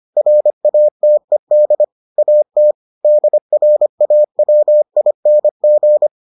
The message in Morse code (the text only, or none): RATED AT DRAWING